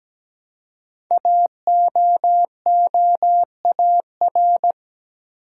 Morse code, heard as a keyed tone, about 17 wpm, 700 Hz. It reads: AOOAR
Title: Code Groups - 5 Letter:Numbers - 17wpm